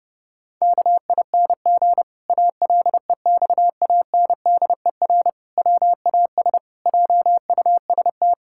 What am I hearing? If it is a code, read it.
Morse code, heard as a keyed tone, about 30 wpm, 700 Hz. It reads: KING ALEXANDER WAS JUST
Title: Pangram Challenge v2 - 30wpm